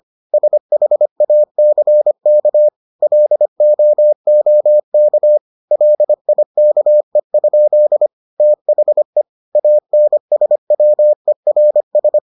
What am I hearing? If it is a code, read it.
Morse code, heard as a keyed tone, about 25 wpm, 600 Hz. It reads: SHACK LOOK LIKE? THE ANSWERS